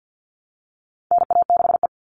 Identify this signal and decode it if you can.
Morse code, heard as a keyed tone, about 48 wpm, 700 Hz. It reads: NA6E
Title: All Cut Numbers - 4-Digit 50wpm